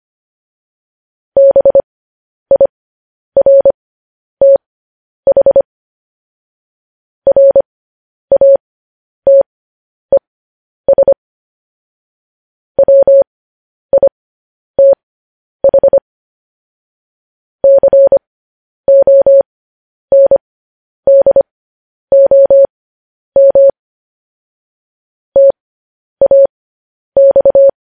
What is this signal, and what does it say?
Morse code, heard as a keyed tone, about 25 wpm, spaced as for 10 wpm, 550 Hz. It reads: BIRTH RATES WITH CONDOM TAX
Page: News Headlines in Morse Code at 10 WPM